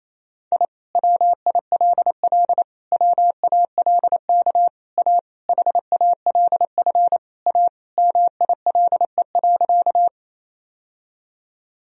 Morse code, heard as a keyed tone, about 27 wpm, 700 Hz. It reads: I WILL WALK A HALF A MILE.
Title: Sentences from Top 1000 Words - Review Part 1 - 28wpm